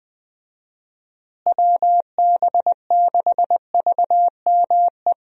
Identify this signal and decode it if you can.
Morse code, heard as a keyed tone, about 20 wpm, 700 Hz. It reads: WB6VME
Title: USA Call Signs - 6 Characters - Repeated 2x - 20wpm